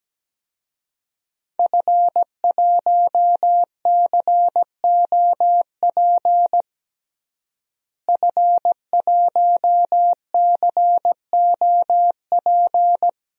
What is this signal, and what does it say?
Morse code, heard as a keyed tone, about 17 wpm, 700 Hz. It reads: F1COP F1COP
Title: DXCC-100 Call Signs - 5 Characters - Repeated 3x - 17wpm